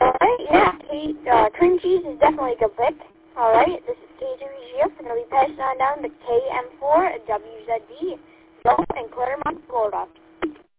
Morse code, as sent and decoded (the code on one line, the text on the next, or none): none